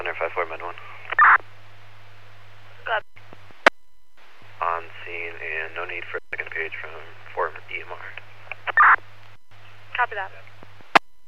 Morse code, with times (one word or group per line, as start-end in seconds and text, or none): none